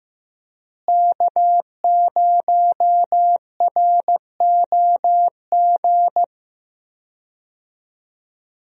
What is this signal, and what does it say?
Morse code, heard as a keyed tone, about 15 wpm, 700 Hz. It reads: K0ROG
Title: USA Call Signs - 5 Characters - Repeated 3x - 15wpm